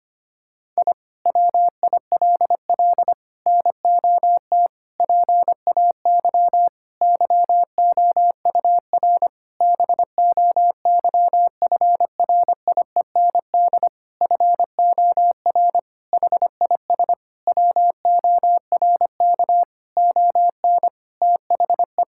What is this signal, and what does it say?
Morse code, heard as a keyed tone, about 25 wpm, 700 Hz. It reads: I WILL NOT PAY YOUR BOYFRIEND FOR HIS WORK ON THE
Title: Sentences from Top 1000 Words - Review Part 5 - 25wpm